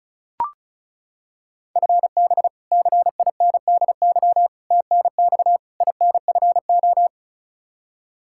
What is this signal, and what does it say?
Morse code, heard as a keyed tone, about 36 wpm, 700 Hz. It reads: FB CINDY TNX INFO
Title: Practice QSO - Exchange 3 - More Info - 35wpm